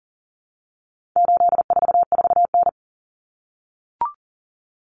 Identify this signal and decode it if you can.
Morse code, heard as a keyed tone, about 40 wpm, 700 Hz. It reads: 844N
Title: Common Cut Numbers - 4-Digit 40wpm